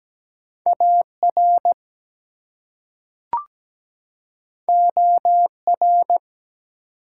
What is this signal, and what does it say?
Morse code, heard as a keyed tone, about 17 wpm, 700 Hz. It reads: AR OR
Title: US States and Canadian Provinces 17wpm